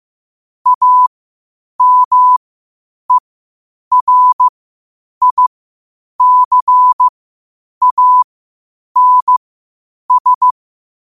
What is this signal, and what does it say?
Morse code, heard as a keyed tone, about 15 wpm, spaced as for 8 wpm, 1000 Hz. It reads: AMERICANS